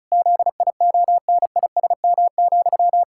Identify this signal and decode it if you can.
Morse code, heard as a keyed tone, about 36 wpm, 700 Hz. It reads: ZIONISM,
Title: Pangram Challenge v2 - 35wpm